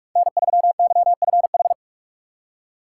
Morse code, heard as a keyed tone, about 47 wpm, 690 Hz. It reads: N3YFH
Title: USA Call Signs - 5 Characters - 2x Character Spacing - 45wpm